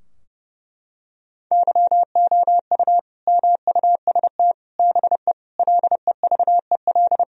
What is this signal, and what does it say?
Morse code, heard as a keyed tone, about 30 wpm, 700 Hz. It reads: YOU MUST BE LEVEL
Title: Sentences from Top 1000 Words - Review Part 4 - 30wpm